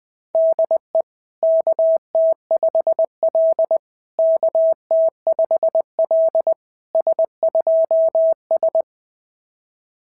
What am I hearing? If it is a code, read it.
Morse code, heard as a keyed tone, about 20 wpm, 690 Hz. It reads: DE KT5L KT5L S2S
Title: SOTA QSO Practice 20wpm